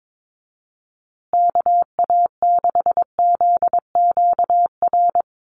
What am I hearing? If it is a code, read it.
Morse code, heard as a keyed tone, about 22 wpm, 700 Hz. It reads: KA6ZQR
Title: USA Call Signs Repeated 3x 22wpm